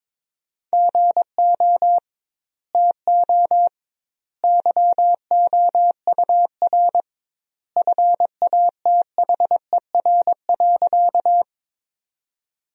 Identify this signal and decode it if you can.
Morse code, heard as a keyed tone, about 22 wpm, 700 Hz. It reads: GO TO YOUR FATHER.
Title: 2x Word-Spacing - Sentences from Top 300 Words - 22wpm